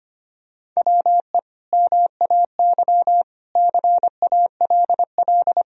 Morse code, heard as a keyed tone, about 25 wpm, 700 Hz. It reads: WE MAY CALL
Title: Sentences from Top 1000 Words - Review Part 1 - 25wpm